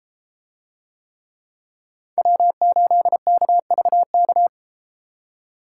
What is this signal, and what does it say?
Morse code, heard as a keyed tone, about 33 wpm, 700 Hz. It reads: W8KVK